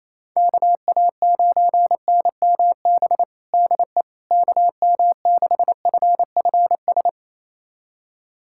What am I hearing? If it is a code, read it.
Morse code, heard as a keyed tone, about 28 wpm, 700 Hz. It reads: KA9NMB DE KM6FFS
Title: USA Callsign DE Callsign 28wpm